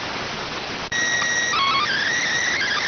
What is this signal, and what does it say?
SSTV (slow-scan television), PD 290.